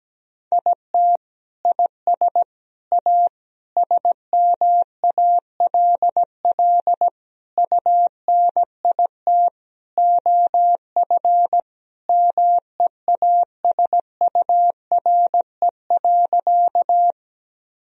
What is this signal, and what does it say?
Morse code, heard as a keyed tone, about 17 wpm, 700 Hz. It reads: IT IS A SMALL UNIT OF MEASURE.
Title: Sentences from Top 1000 Words - Review Part 2 - 17wpm